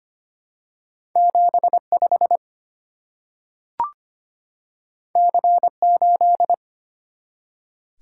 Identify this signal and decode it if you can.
Morse code, heard as a keyed tone, about 25 wpm, 700 Hz. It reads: 75 C8
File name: Two Letter-Number Combinations - 25wpm